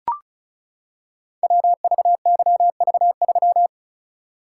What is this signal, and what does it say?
Morse code, heard as a keyed tone, about 36 wpm, 700 Hz. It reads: WVYV3
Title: Code Groups - 5 Letter:Numbers - 35wpm